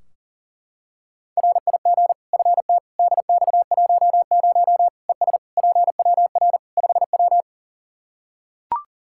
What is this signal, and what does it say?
Morse code, heard as a keyed tone, about 40 wpm, 700 Hz. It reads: RIG FT DX10 ES PWR 5W
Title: Practice QSO - Exchange 2 - Info - 40wpm